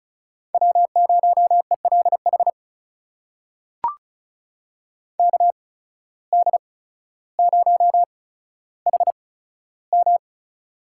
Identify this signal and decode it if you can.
Morse code, heard as a keyed tone, about 34 wpm, spaced as for 15 wpm, 700 Hz. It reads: W0ELH KD0HM
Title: USA Call Signs - 5 Characters - 8x Character Spacing - 35wpm